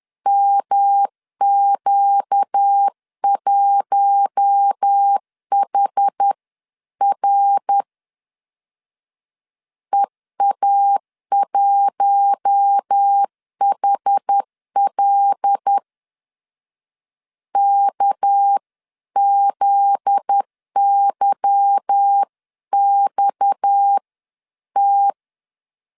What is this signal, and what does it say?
Morse code, heard as a keyed tone, about 10 wpm, 800 Hz. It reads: MQ1HR EA1HL KZYXT
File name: LEZIONI / 8 YJ45 / 8 YJ4 10wpm 2633